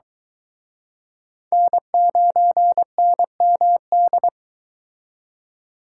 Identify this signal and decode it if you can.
Morse code, heard as a keyed tone, about 23 wpm, 700 Hz. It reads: N9NMD